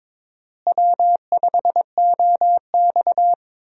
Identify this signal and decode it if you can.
Morse code, heard as a keyed tone, about 22 wpm, 700 Hz. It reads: W5OX
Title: USA Call Signs - 4 Characters - Repeated 2x - 22wpm